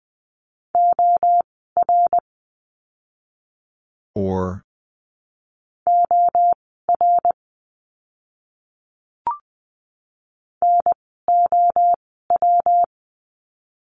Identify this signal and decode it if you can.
Morse code, heard as a keyed tone, about 20 wpm, spaced as for 14 wpm, 700 Hz. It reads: OR OR NOW
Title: 192 Lesson 31f - Letter v - Words 20wpm